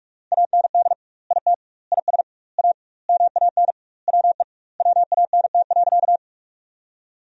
Morse code, heard as a keyed tone, about 42 wpm, 700 Hz. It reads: AND IT IS A MAN WE WANT.
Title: 2x Word-Spacing - Sentences from Top 200 Words - 45wpm